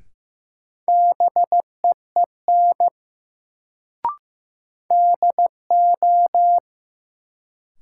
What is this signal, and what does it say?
Morse code, heard as a keyed tone, about 15 wpm, 700 Hz. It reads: BEEN DO
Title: Sets of 2 Words - New Word Drill - 15wpm